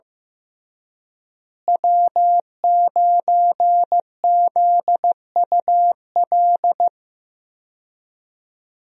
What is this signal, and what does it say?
Morse code, heard as a keyed tone, about 15 wpm, 700 Hz. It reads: W9ZUL
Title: USA Call Signs - 5 Characters - Repeated 3x - 15wpm